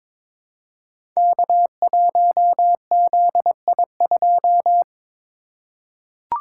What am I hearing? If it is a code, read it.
Morse code, heard as a keyed tone, about 22 wpm, 700 Hz. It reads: K1ZI2
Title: Code Groups - 5 Letter:Numbers - 22wpm